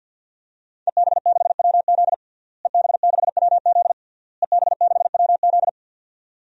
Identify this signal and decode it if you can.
Morse code, heard as a keyed tone, about 49 wpm, 700 Hz. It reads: EB6WZ EB6WZ EB6WZ
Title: DXCC-100 Call Signs - 5 Characters - Repeated 3x - 50wpm